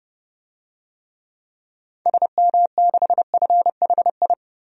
Morse code, heard as a keyed tone, about 30 wpm, 700 Hz. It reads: SM6FHI
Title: DXCC-100 Call Signs - 6 Characters - 30wpm